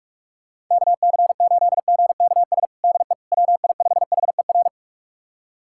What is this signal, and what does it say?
Morse code, heard as a keyed tone, about 46 wpm, 700 Hz. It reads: KC8GKS DE WI5HER
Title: USA Callsign DE Callsign 45wpm